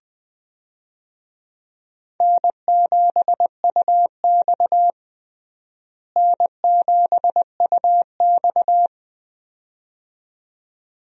Morse code, heard as a keyed tone, about 21 wpm, 700 Hz. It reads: N7UX N7UX
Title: USA Call Signs - 4 Characters - Repeated 2x - 20wpm